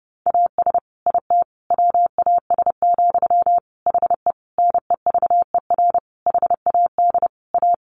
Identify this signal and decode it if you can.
Morse code, heard as a keyed tone, about 30 wpm, 700 Hz. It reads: AS IT WAS, HE NEVER HAD A
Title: Sentences from Top 1400 Words 30wpm